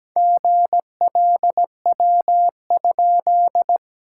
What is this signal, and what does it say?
Morse code, heard as a keyed tone, about 17 wpm, 700 Hz. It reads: GLW?
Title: SOTA QSO Practice 17wpm